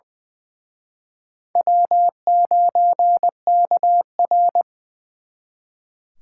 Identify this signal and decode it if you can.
Morse code, heard as a keyed tone, about 25 wpm, 690 Hz. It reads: W9KR
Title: Speed Racing - USA 1x2 Call Signs 20wpm